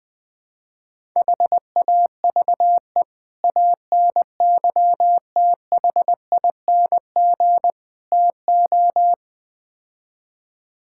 Morse code, heard as a keyed tone, about 21 wpm, 700 Hz. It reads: HAVE ANYTHING TO
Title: Build A Phrase - 5-Word Phrases - 20wpm